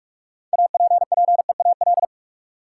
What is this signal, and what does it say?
Morse code, heard as a keyed tone, about 44 wpm, 700 Hz. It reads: APPEAL
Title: Sentences from Top 1300 Words - New Word Drill Alt - 45wpm